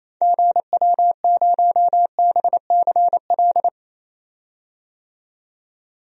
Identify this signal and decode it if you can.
Morse code, heard as a keyed tone, about 28 wpm, 700 Hz. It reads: GW0BCL
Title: DXCC-100 Call Signs - 6 Characters - 28wpm